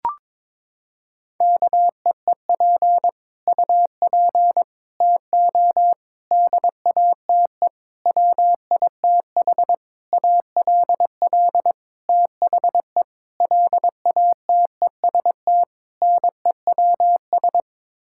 Morse code, heard as a keyed tone, about 22 wpm, 700 Hz. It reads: KEEP UP TO DATE WITH ALL THE LATEST NEWS
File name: Sets of 9 Words - 22wpm